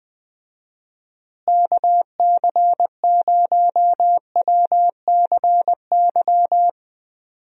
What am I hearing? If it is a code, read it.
Morse code, heard as a keyed tone, about 20 wpm, 700 Hz. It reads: KC0WCY
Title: USA Call Signs - 6 Characters - Repeated 3x - 20wpm